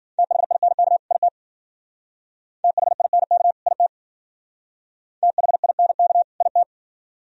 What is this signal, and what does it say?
Morse code, heard as a keyed tone, about 52 wpm, 700 Hz. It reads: THINK IT THINK IT THINK IT